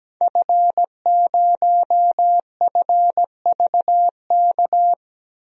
Morse code, heard as a keyed tone, about 17 wpm, 700 Hz. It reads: F0FVK
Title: DXCC-100 Call Signs - 5 Characters - Repeated 3x - 17wpm